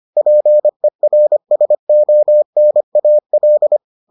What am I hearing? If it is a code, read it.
Morse code, heard as a keyed tone, about 25 wpm, 600 Hz. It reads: PERSONAL